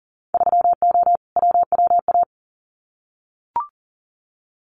Morse code, heard as a keyed tone, about 40 wpm, 700 Hz. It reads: 3O WWA
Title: Winter Field Day Exchanges Encore 40wpm